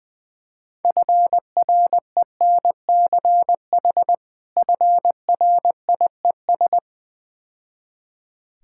Morse code, heard as a keyed tone, about 20 wpm, 700 Hz. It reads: FRENCH FRIES
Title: Difficult Words - Speed-Racing with Hints - 20wpm